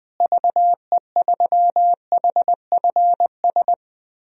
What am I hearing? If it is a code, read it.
Morse code, heard as a keyed tone, about 20 wpm, 700 Hz. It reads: VE3HFS
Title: DXCC-100 Call Signs - 6 Characters - Repeated 3x - 20wpm